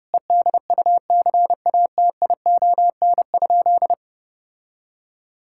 Morse code, heard as a keyed tone, about 30 wpm, 700 Hz. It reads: EDUCATION?
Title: Sentences from Top 1600 Words 30wpm